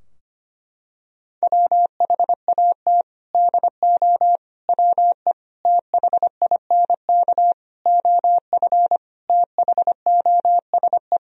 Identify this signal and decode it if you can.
Morse code, heard as a keyed tone, about 25 wpm, 700 Hz. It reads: WHAT DO WE THINK OF THOSE
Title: Sentences from Top 1000 Words - Review Part 2 - 25wpm